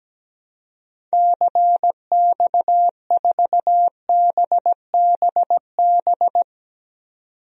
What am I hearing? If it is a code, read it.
Morse code, heard as a keyed tone, about 17 wpm, 700 Hz. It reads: CX4BBB
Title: DXCC-100 Call Signs - 6 Characters - 17wpm